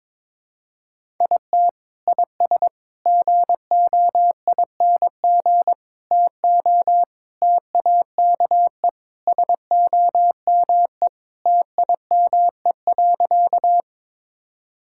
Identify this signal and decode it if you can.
Morse code, heard as a keyed tone, about 22 wpm, 700 Hz. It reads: IT IS GOING TO TAKE SOME TIME.
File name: practice / 22wpm / Sentences with 3-Word Phrases - 22wpm